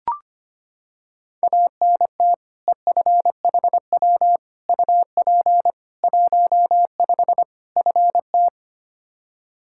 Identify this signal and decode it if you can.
Morse code, heard as a keyed tone, about 25 wpm, 700 Hz. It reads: ANT EFHW UP 15 FT
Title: Practice QSO - Exchange 2 - Info - 25wpm